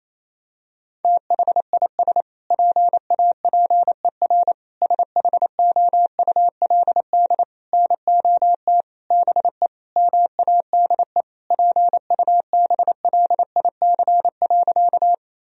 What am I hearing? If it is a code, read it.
Morse code, heard as a keyed tone, about 28 wpm, 700 Hz. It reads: THIS PAPER SHOULD NOT BE MADE PUBLIC.
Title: Sentences from Top 1000 Words - Review Part 3 - 28wpm